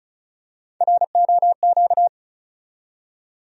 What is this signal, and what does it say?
Morse code, heard as a keyed tone, about 36 wpm, 700 Hz. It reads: ROQ
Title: USA Call Sign Postfixes - 3 Letters 35wpm